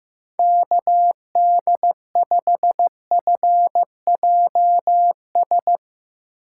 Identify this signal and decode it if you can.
Morse code, heard as a keyed tone, about 16 wpm, 700 Hz. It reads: KD5FJS
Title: Speed Racing - USA 2x3 Call Signs 15wpm